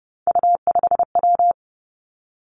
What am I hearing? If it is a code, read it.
Morse code, heard as a keyed tone, about 30 wpm, 700 Hz. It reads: U5W
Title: Three Challenging Letter-Number Combinations 30wpm